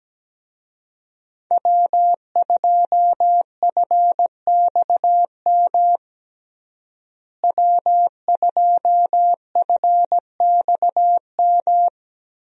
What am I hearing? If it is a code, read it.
Morse code, heard as a keyed tone, about 17 wpm, 700 Hz. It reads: W2FXM W2FXM
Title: USA Call Signs - 5 Characters - Repeated 2x - 17wpm